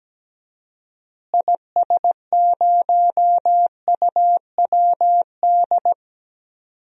Morse code, heard as a keyed tone, about 17 wpm, 700 Hz. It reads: IS0UWD
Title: DXCC-100 Call Signs - 6 Characters - Repeated 2x - 17wpm